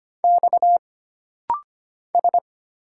Morse code, heard as a keyed tone, about 25 wpm, 700 Hz. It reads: X S
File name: Single Character (Letter:Number:Punctuation:Prosign) - Rapid-Fire - 25wpm